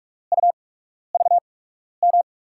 Morse code, heard as a keyed tone, about 43 wpm, spaced as for 12 wpm, 700 Hz. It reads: UVM
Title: USA Call Signs - 6 Characters - 8x Character Spacing - 45wpm